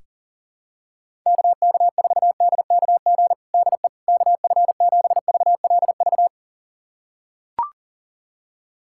Morse code, heard as a keyed tone, about 37 wpm, 700 Hz. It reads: KK4DKG DE KF7VLV
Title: USA Callsign DE Callsign 40wpm